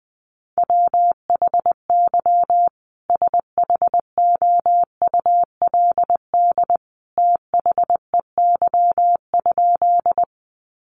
Morse code, heard as a keyed tone, about 20 wpm, 700 Hz. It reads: WHY SHOULD THEY?
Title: Sentences from Top 300 Words 20wpm